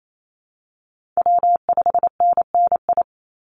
Morse code, heard as a keyed tone, about 28 wpm, 700 Hz. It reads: W5NNI
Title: USA Call Signs 28wpm